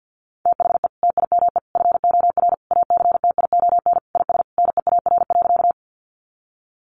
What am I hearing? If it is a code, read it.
Morse code, heard as a keyed tone, about 50 wpm, 700 Hz. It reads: THE TIME FOR ACTION IS NEAR.
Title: Sentences from Top 700 Words 50wpm